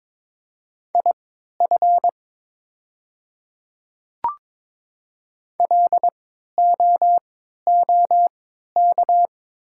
Morse code, heard as a keyed tone, about 22 wpm, spaced as for 12 wpm, 700 Hz. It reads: IF LOOK
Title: 3x Character Spacing - Top 100 Words - 22wpm